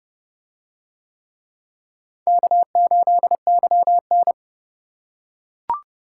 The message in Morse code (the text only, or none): K8YN